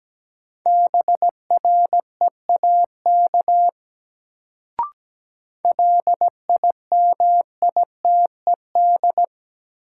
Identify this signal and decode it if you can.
Morse code, heard as a keyed tone, about 17 wpm, 700 Hz. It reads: BREAK LIMITED
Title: Sentences from Top 900 Words - New Word Drill - 17wpm